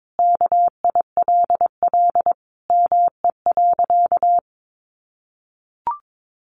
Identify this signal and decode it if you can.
Morse code, heard as a keyed tone, about 22 wpm, 700 Hz. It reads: KILL ME.
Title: Sentences from Top 1400 Words 22wpm